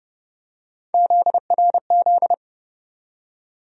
Morse code, heard as a keyed tone, about 30 wpm, 700 Hz. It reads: ZRZ
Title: USA Call Sign Postfixes - 3 Letters 30wpm